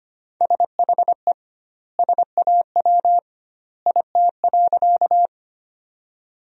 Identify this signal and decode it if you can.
Morse code, heard as a keyed tone, about 25 wpm, 700 Hz. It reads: SHE SAW IT.
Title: 2x Word-Spacing - Sentences from Top 300 Words - 25wpm